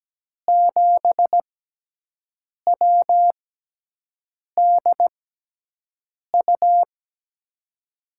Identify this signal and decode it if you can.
Morse code, heard as a keyed tone, about 17 wpm, spaced as for 6 wpm, 700 Hz. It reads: 7WDU
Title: USA Call Signs - 6 Characters - 6x Character Spacing - 17wpm